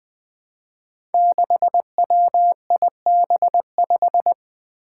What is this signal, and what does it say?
Morse code, heard as a keyed tone, about 20 wpm, 700 Hz. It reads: 6WIB5
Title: Five Challenging Letter-Number Combinations 20wpm